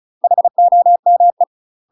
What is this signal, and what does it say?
Morse code, heard as a keyed tone, about 34 wpm, 700 Hz. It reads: HOME
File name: UK MASTER QSOs TRAINING / 35WPM / UK MASTER QSOs 0004